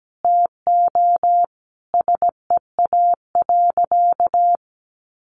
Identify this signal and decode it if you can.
Morse code, heard as a keyed tone, about 17 wpm, 700 Hz. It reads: TO SEA.
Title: Sentences from Top 400 Words 17wpm